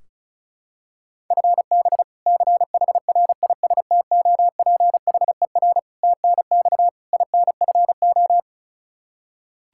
Morse code, heard as a keyed tone, about 34 wpm, 700 Hz. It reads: FB CHRISTOPHER TNX INFO
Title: K1USN SST QSO Practice 35wpm